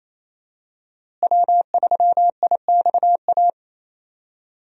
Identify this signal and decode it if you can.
Morse code, heard as a keyed tone, about 28 wpm, 700 Hz. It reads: W3IXA
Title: USA Call Signs - 5 Characters - Repeated 3x - 28wpm